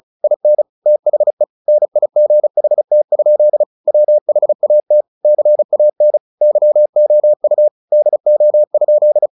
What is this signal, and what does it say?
Morse code, heard as a keyed tone, about 34 wpm, 600 Hz. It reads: IN THE NIGHT? WHAT CAN YOU DO?
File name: the old man and the sea 2 35wpm